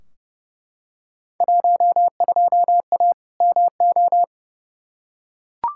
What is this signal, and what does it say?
Morse code, heard as a keyed tone, about 30 wpm, 700 Hz. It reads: 12A MO
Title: Field Day Exchanges Encore - 30wpm